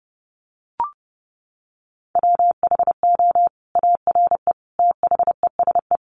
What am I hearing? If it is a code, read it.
Morse code, heard as a keyed tone, about 30 wpm, 700 Hz. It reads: WHO ARE THESE
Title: Sentences from Top 200 Words 30wpm